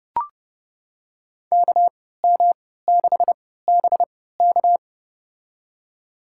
Morse code, heard as a keyed tone, about 30 wpm, spaced as for 17 wpm, 700 Hz. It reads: KM6BK
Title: USA Call Signs - 3x Character Spacing - 30wpm